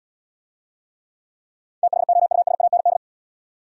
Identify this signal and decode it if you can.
Morse code, heard as a keyed tone, about 81 wpm, 690 Hz. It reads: THOUSAND